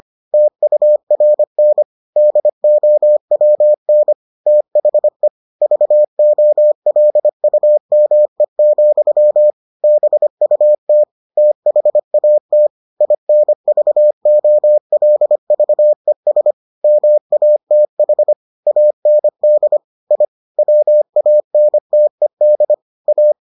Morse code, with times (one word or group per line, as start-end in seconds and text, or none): 0.3-1.8 TURN
2.2-4.1 DOWN
4.5-5.3 THE
5.6-9.5 VOLUME,
9.8-11.0 BUT
11.4-12.7 THAT
13.0-16.5 INVOLVES
16.8-18.3 MATH
18.7-19.8 AND
20.1-20.2 I
20.6-22.7 WANTED
23.1-23.3 A